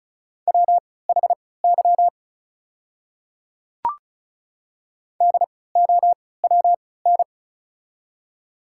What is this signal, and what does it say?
Morse code, heard as a keyed tone, about 36 wpm, spaced as for 20 wpm, 700 Hz. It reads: WHY DOWN
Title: 3x Character Spacing - Top 200 Words - 35wpm